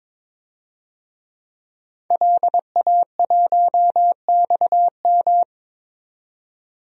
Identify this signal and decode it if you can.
Morse code, heard as a keyed tone, about 22 wpm, 700 Hz. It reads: LA1XM
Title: DXCC-100 Call Signs - 5 Characters - 22wpm